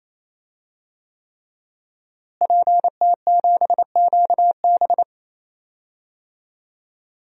Morse code, heard as a keyed tone, about 27 wpm, 700 Hz. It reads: PT7QB
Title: DXCC-100 Call Signs - 5 Characters - Repeated 3x - 28wpm